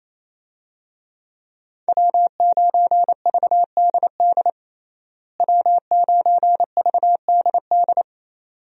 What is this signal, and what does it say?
Morse code, heard as a keyed tone, about 28 wpm, 700 Hz. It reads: W9VDD W9VDD
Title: USA Call Signs - 5 Characters - Repeated 2x - 28wpm